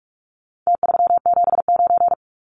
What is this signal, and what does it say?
Morse code, heard as a keyed tone, about 44 wpm, 700 Hz. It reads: T379
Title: Numbers - Contest Style - 4-Digit 45wpm